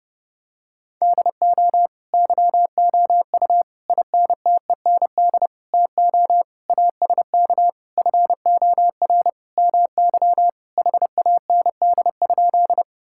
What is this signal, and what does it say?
Morse code, heard as a keyed tone, about 31 wpm, 700 Hz. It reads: DO YOU INTEND TO ASK FOR MY HAND?
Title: Sentences from Top 1600 Words 30wpm